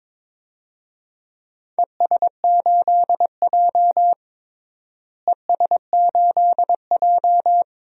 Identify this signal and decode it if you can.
Morse code, heard as a keyed tone, about 22 wpm, 700 Hz. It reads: ES8J ES8J
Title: DXCC-100 Call Signs - 4 Characters - Repeated 2x - 22wpm